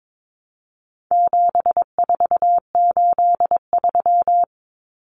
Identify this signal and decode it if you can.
Morse code, heard as a keyed tone, about 22 wpm, 700 Hz. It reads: 7483
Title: Common Cut Numbers - 4-Digit 22wpm